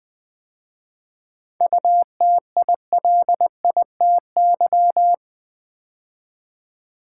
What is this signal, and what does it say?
Morse code, heard as a keyed tone, about 20 wpm, 700 Hz. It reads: UTILITY
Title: 7 Letter Words - First 3 Letters Given - 20wpm